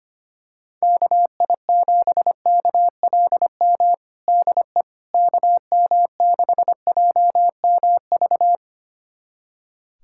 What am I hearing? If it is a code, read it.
Morse code, heard as a keyed tone, about 25 wpm, 700 Hz. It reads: KI7KLM DE KM6JMV
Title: USA Callsign DE Callsign - 6 Characters - 25wpm